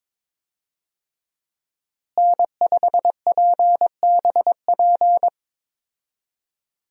Morse code, heard as a keyed tone, about 22 wpm, 700 Hz. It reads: N5PBP
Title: USA Call Signs - 5 Characters - Repeated 3x - 22wpm